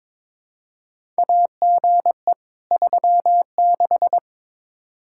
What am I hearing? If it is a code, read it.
Morse code, heard as a keyed tone, about 22 wpm, 700 Hz. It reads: AGE 36
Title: Practice QSO - Exchange 3 - More Info - 22wpm